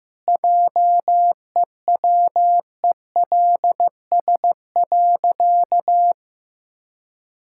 Morse code, heard as a keyed tone, about 15 wpm, 700 Hz. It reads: JEWELS.
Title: Pangram Challenge v2 - 15wpm